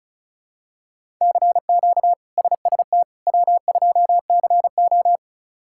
Morse code, heard as a keyed tone, about 36 wpm, 700 Hz. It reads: CQ SST W2CO
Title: K1USN SST QSO Practice - 35wpm